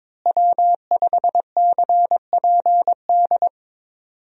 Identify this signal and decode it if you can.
Morse code, heard as a keyed tone, about 22 wpm, 700 Hz. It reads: W5CPD
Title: USA Call Signs - 5 Characters - Repeated 3x - 22wpm